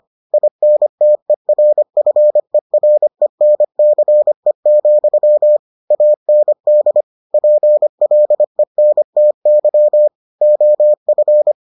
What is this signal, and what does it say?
Morse code, heard as a keyed tone, about 25 wpm, 600 Hz. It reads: INTERFERENCE, AND PLENTY OF